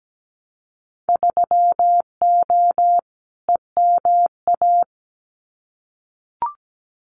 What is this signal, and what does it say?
Morse code, heard as a keyed tone, about 17 wpm, 700 Hz. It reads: 3O EMA
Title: Winter Field Day Exchanges 17wpm